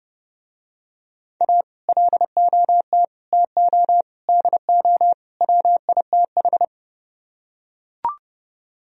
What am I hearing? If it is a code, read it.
Morse code, heard as a keyed tone, about 30 wpm, 700 Hz. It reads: A LOT TO DO WITH